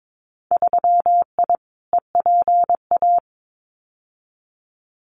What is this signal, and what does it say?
Morse code, heard as a keyed tone, about 22 wpm, 700 Hz. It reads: 3I EPA
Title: Winter Field Day Exchanges Encore 22wpm